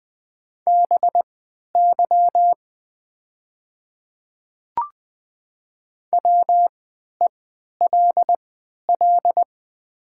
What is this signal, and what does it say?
Morse code, heard as a keyed tone, about 20 wpm, spaced as for 11 wpm, 700 Hz. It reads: BY WELL
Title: 3x Character Spacing - Top 300 Words - 20wpm